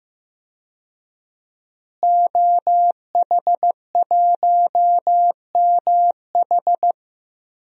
Morse code, heard as a keyed tone, about 15 wpm, 700 Hz. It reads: OH1MH
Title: DXCC-100 Call Signs - 5 Characters - Repeated 2x - 15wpm